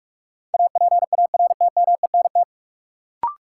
Morse code, heard as a keyed tone, about 46 wpm, 700 Hz. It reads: APARTMENT
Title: Sentences from Top 900 Words - New Word Drill - 45wpm